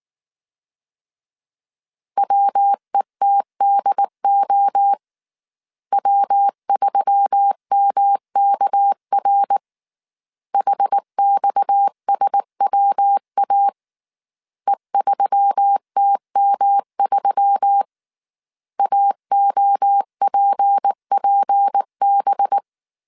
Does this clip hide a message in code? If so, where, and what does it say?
Yes: Morse code from 2.2 to 22.6 s, heard as a keyed tone, about 19 wpm, 800 Hz. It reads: WETDO W3MKR HXSWA E3TM3 AOPPB